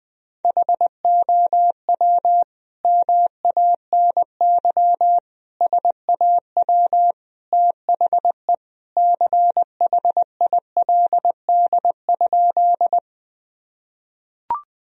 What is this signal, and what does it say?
Morse code, heard as a keyed tone, about 21 wpm, 700 Hz. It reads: HOW MANY SAW THE CHILD?
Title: Sentences from Top 1000 Words - Review Part 1 - 20wpm